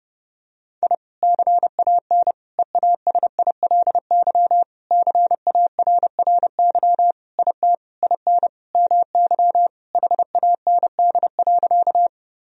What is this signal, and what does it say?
Morse code, heard as a keyed tone, about 30 wpm, 700 Hz. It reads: I CAN EASILY CARRY IT IN MY HAND.
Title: Sentences from Top 1000 Words - Review Part 1 - 30wpm